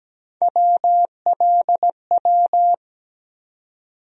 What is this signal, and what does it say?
Morse code, heard as a keyed tone, about 17 wpm, 700 Hz. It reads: WLW
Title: USA Call Sign Postfixes - 3 Letters 17wpm